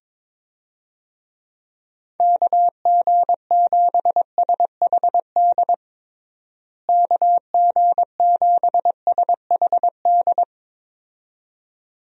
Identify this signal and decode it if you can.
Morse code, heard as a keyed tone, about 22 wpm, 700 Hz. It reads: KG7SHD KG7SHD
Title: USA Call Signs - 6 Characters - Repeated 2x - 22wpm